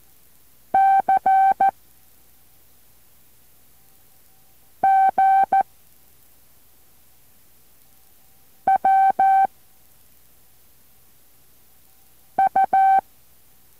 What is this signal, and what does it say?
Morse code, heard as a keyed tone, about 14 wpm, 770 Hz. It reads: C G W U
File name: intl morse code2